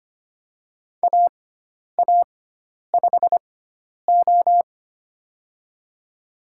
Morse code, heard as a keyed tone, about 25 wpm, spaced as for 10 wpm, 700 Hz. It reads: AA5O
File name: Lesson 16g - Letter c - Call Signs 25wpm